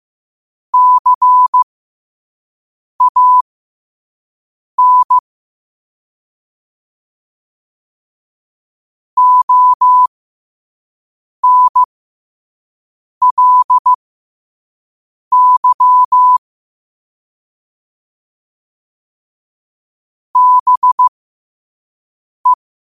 Morse code, heard as a keyed tone, about 15 wpm, spaced as for 5 wpm, 1000 Hz. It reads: CAN ONLY BE
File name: qotdfiles / 5wpm